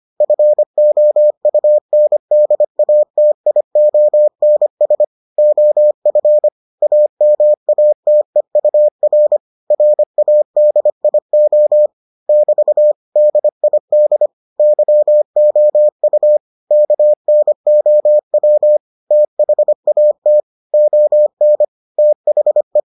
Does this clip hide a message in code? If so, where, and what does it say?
Yes: Morse code from 0.2 to 22.8 s, heard as a keyed tone, about 25 wpm, 600 Hz. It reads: FOUNDATIONS OF AMATEUR RADIO <BT> DID YOU KNOW THAT ON THE